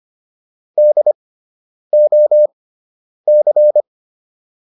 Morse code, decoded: DOC

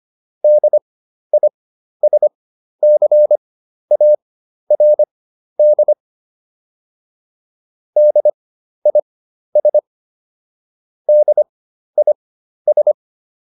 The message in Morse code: DISCARD DIS DIS

